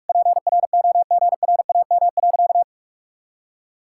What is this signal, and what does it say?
Morse code, heard as a keyed tone, about 46 wpm, 700 Hz. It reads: PROGRAM.
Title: Sentences from Top 1000 Words - Review Part 5 - 45wpm